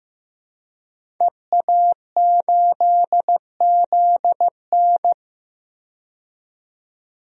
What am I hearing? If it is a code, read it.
Morse code, heard as a keyed tone, about 15 wpm, 700 Hz. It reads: EA8ZN
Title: DXCC-100 Call Signs - 5 Characters - 15wpm